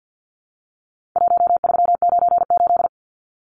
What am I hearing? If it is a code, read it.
Morse code, heard as a keyed tone, about 50 wpm, 700 Hz. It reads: 1398
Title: Numbers - Contest Style - 4-Digit 50wpm